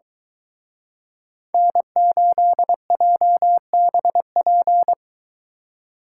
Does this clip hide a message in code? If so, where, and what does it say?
Yes: Morse code from 1.5 to 4.9 s, heard as a keyed tone, about 23 wpm, 700 Hz. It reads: N8JBP